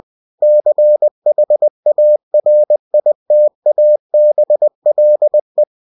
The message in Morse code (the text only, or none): CHARITABLE